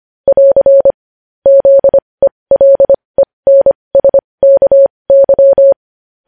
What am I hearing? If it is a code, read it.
Morse code, heard as a keyed tone, about 25 wpm, spaced as for 20 wpm, 550 Hz. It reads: <AR> ZELENSKY
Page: News Headlines in Morse Code at 20 WPM